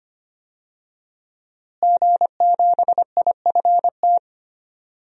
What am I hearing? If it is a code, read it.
Morse code, heard as a keyed tone, about 25 wpm, 700 Hz. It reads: G7IFT
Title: DXCC-100 Call Signs - 5 Characters - Repeated 3x - 25wpm